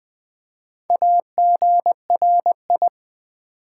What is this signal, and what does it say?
Morse code, heard as a keyed tone, about 20 wpm, 700 Hz. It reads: AGRI
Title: Build A Word - 4 Letters At A Time - 11-Letter Words - 20wpm